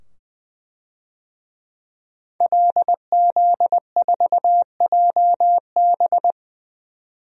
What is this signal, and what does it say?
Morse code, heard as a keyed tone, about 21 wpm, 700 Hz. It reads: LZ4JB
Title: DXCC-100 Call Signs - 5 Characters - 20wpm